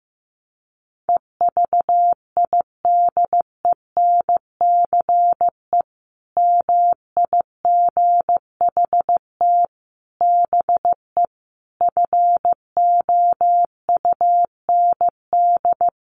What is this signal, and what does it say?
Morse code, heard as a keyed tone, about 15 wpm, 700 Hz. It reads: EVIDENCE MIGHT BE FOUND
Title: Sentences from Top 1300 Words 15wpm